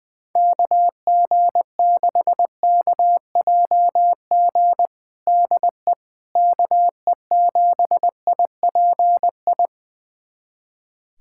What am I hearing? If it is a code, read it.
Morse code, heard as a keyed tone, about 20 wpm, 700 Hz. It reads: KG6KJG DE KE7IPI
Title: USA Callsign DE Callsign - 6 Characters - 20wpm